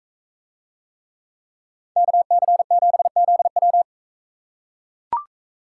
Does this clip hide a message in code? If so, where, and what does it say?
Yes: Morse code from 2.0 to 3.8 s, heard as a keyed tone, about 42 wpm, 700 Hz. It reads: KC7ZW